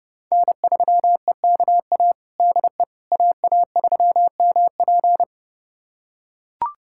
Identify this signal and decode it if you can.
Morse code, heard as a keyed tone, about 32 wpm, 700 Hz. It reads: N3EKA DE AA3MP
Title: USA Callsign DE Callsign - 5 Characters - 30wpm